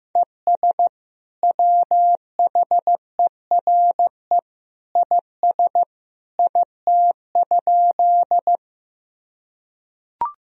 Morse code, heard as a keyed tone, about 14 wpm, 700 Hz. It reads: ES WHERE IS IT?
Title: Sentences from Top 200 Words and Abbreviations 15wpm